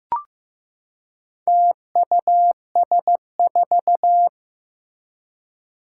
Morse code, heard as a keyed tone, about 15 wpm, 700 Hz. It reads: TUS4
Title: Four Challenging Letter-Number Combinations 15wpm